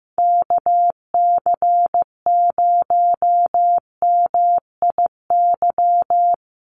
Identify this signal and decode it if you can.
Morse code, heard as a keyed tone, about 15 wpm, 700 Hz. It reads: KC0MIY
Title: USA Call Signs Repeated 3x 15wpm